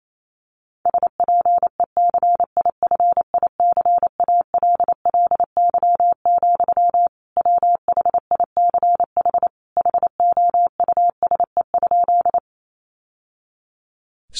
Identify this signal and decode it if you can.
Morse code, heard as a keyed tone, about 28 wpm, 700 Hz. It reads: SPECIFICALLY, WHICH HOUSE?
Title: Sentences from Top 1100 Words 28wpm